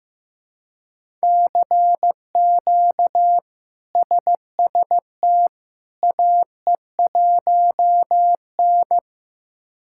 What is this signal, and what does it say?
Morse code, heard as a keyed tone, about 15 wpm, 700 Hz. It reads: CQ SST AE1N
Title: K1USN SST QSO Practice - 15wpm